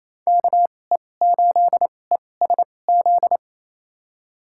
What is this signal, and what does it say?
Morse code, heard as a keyed tone, about 28 wpm, spaced as for 20 wpm, 700 Hz. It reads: KE8ESZ
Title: USA Call Signs - 6 Characters - 2x Character Spacing - 28wpm